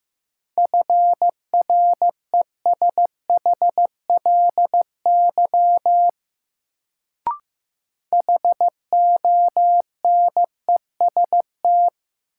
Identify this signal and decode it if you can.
Morse code, heard as a keyed tone, about 16 wpm, 700 Hz. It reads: FRESHLY HONEST
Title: Sets of 5 Words Encore - New Word Drill - 15wpm